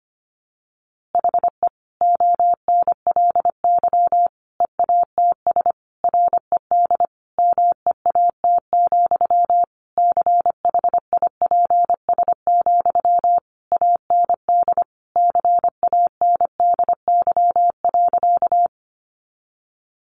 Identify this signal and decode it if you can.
Morse code, heard as a keyed tone, about 25 wpm, 700 Hz. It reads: HE ONLY EATS RED MEAT, CHIPS, AND CANDY.
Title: Easy Phrases 25wpm